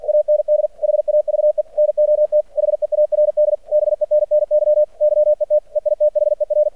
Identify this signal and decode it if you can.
Morse code, heard as a keyed tone, about 48 wpm, 600 Hz. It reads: AND HAVE NOT HEARD BENNY YET EITHER
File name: archived / CRSnet